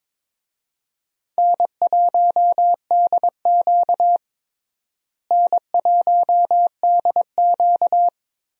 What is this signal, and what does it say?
Morse code, heard as a keyed tone, about 22 wpm, 700 Hz. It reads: N1DQ N1DQ
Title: USA Call Signs - 4 Characters - Repeated 2x - 22wpm